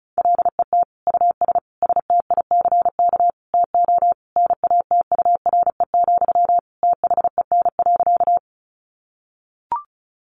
Morse code, heard as a keyed tone, about 34 wpm, 700 Hz. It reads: LET US STICK TO NATURE, THEN.
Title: Sentences from Top 1200 Words 35wpm